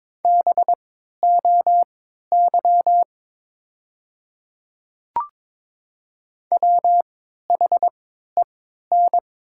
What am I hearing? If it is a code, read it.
Morse code, heard as a keyed tone, about 22 wpm, spaced as for 13 wpm, 700 Hz. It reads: BOY WHEN
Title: 3x Character Spacing - Top 300 Words - 22wpm